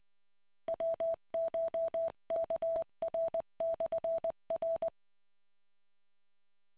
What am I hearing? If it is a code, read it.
Morse code, heard as a keyed tone, about 20 wpm, 660 Hz. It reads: WCHKR/R